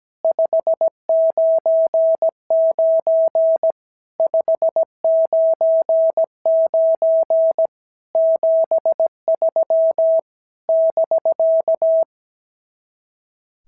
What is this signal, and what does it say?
Morse code, heard as a keyed tone, about 17 wpm, 650 Hz. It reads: 599 599 73 <BK>
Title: SOTA QSO Practice 17wpm